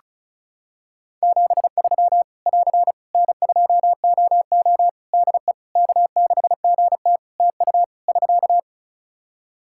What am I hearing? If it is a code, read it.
Morse code, heard as a keyed tone, about 36 wpm, 700 Hz. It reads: 73 <AR> N2OO DE K6GT TU <SK>
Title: Practice QSO - Exchange 4 - Ending - 35wpm